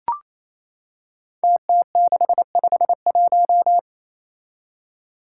Morse code, heard as a keyed tone, about 28 wpm, 700 Hz. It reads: TT651
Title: Five Challenging Letter-Number Combinations 28wpm